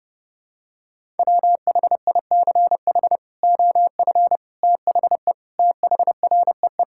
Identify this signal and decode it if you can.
Morse code, heard as a keyed tone, about 30 wpm, 700 Hz. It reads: WHICH OF THE THREE
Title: Sentences from Top 1000 Words - Review Part 5 - 30wpm